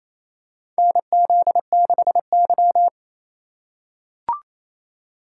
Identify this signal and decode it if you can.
Morse code, heard as a keyed tone, about 27 wpm, 700 Hz. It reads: NZ6Y
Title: USA Call Signs - 4 Characters - 3x Character Spacing - 28wpm